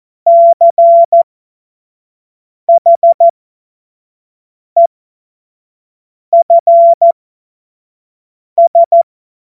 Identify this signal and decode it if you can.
Morse code, heard as a keyed tone, about 14 wpm, spaced as for 5 wpm, 680 Hz. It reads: CHEFS
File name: Morsekurs / Witze / Witz 31. Januar 2023 25 F70